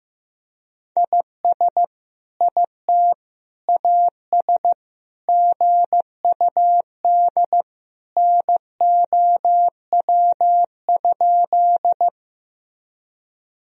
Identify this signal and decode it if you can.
Morse code, heard as a keyed tone, about 16 wpm, 700 Hz. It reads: IS IT AS GUD NOW?
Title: Sentences from Top 200 Words and Abbreviations 15wpm